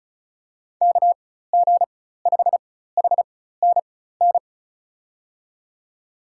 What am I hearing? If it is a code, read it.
Morse code, heard as a keyed tone, about 34 wpm, spaced as for 16 wpm, 700 Hz. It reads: KG5HNN
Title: USA Call Signs - 6 Characters - 4x Character Spacing - 35wpm